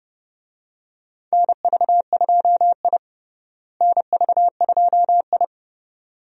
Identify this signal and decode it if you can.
Morse code, heard as a keyed tone, about 30 wpm, 700 Hz. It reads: NV2I NV2I
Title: USA Call Signs - 4 Characters - Repeated 2x - 30wpm